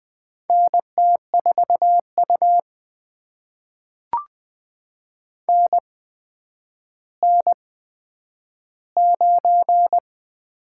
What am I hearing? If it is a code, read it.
Morse code, heard as a keyed tone, about 20 wpm, spaced as for 10 wpm, 700 Hz. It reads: NT4U NN9
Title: USA Call Signs - 4 Characters - 8x Character Spacing - 20wpm